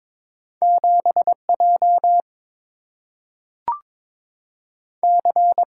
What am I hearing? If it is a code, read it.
Morse code, heard as a keyed tone, about 22 wpm, 700 Hz. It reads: 7 J C
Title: Single-Letter Single-Number 22wpm